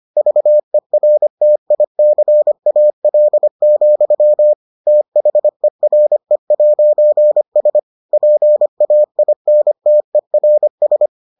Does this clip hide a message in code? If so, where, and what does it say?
Yes: Morse code from 0.2 to 11.1 s, heard as a keyed tone, about 25 wpm, 600 Hz. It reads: VERTICAL, THERE'S PAINTERS